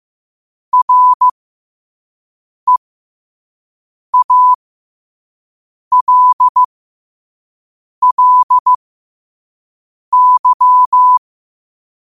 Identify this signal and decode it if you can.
Morse code, heard as a keyed tone, about 15 wpm, spaced as for 5 wpm, 1000 Hz. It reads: REALLY
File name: qotdfiles / 5wpm